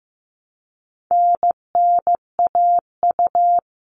Morse code, heard as a keyed tone, about 15 wpm, 700 Hz. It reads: NNAU
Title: All Cut Numbers - 4-Digit 15wpm